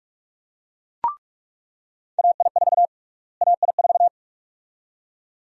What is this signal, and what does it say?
Morse code, heard as a keyed tone, about 42 wpm, 700 Hz. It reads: AI4 AI4
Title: USA Call Sign Prefixes - Repeated 2x - 45wpm